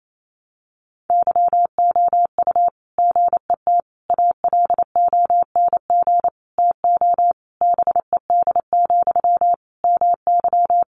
Morse code, heard as a keyed tone, about 28 wpm, 700 Hz. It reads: YOU GET ALONG TO BED, MY
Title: Sentences from Top 700 Words 28wpm